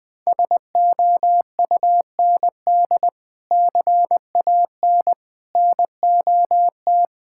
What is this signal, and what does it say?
Morse code, heard as a keyed tone, about 20 wpm, 700 Hz. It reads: SOUND CAN NOT
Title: Sentences from Top 1000 Words - Review Part 6 - 20wpm